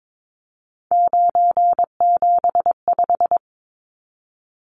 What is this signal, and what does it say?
Morse code, heard as a keyed tone, about 22 wpm, 700 Hz. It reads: 975
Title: Numbers - 3-Digit 22wpm